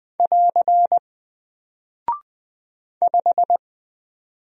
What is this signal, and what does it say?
Morse code, heard as a keyed tone, about 20 wpm, 700 Hz. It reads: <AR> 5
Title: Single Character (Letter:Number:Punctuation:Prosign) Alt 20wpm